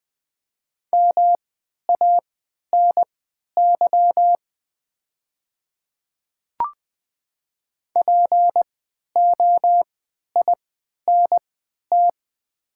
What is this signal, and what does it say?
Morse code, heard as a keyed tone, about 20 wpm, spaced as for 11 wpm, 700 Hz. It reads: MANY POINT
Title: 3x Character Spacing - Top 300 Words - 20wpm